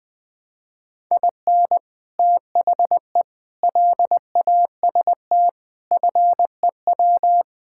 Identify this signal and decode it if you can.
Morse code, heard as a keyed tone, about 20 wpm, 700 Hz. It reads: IN THE LAST FEW
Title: Build A Phrase - 5-Word Phrases - 20wpm